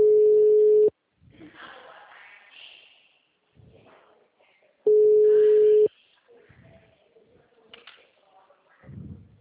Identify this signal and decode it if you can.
Ringback tone (Europe, 425 Hz).